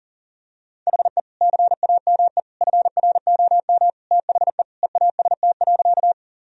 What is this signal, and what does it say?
Morse code, heard as a keyed tone, about 40 wpm, 700 Hz. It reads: HE CAME FROM THE EAST.
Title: Sentences from Top 1000 Words - Review Part 3 - 40wpm